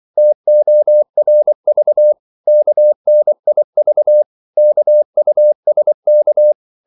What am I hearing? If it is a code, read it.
Morse code, heard as a keyed tone, about 24 wpm, 600 Hz. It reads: TORV KNIV KUSK
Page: Korte danske ord 24wpm